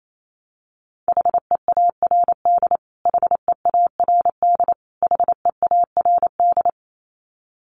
Morse code, heard as a keyed tone, about 28 wpm, 700 Hz. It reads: HEARD HEARD HEARD
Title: Top 500 Words Repeated 28wpm